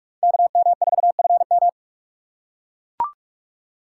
Morse code, heard as a keyed tone, about 43 wpm, 700 Hz. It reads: KM4FM